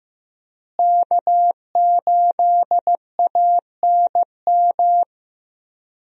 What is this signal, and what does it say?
Morse code, heard as a keyed tone, about 15 wpm, 700 Hz. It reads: K8ANM
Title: USA Call Signs - 5 Characters - Repeated 3x - 15wpm